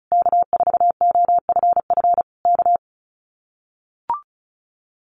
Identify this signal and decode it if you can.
Morse code, heard as a keyed tone, about 34 wpm, 700 Hz. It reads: K4OFF K
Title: Common QSO Phrases Encore 35wpm